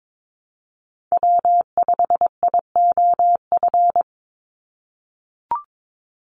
Morse code, heard as a keyed tone, about 22 wpm, 700 Hz. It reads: W5IOF
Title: USA Call Signs 22wpm